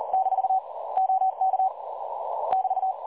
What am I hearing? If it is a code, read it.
Morse code, heard as a keyed tone, about 38 wpm, 760 Hz. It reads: - OK <BT>
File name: dlazaawansowanych / nagraniazpasma / websdr recording 3565.0kHz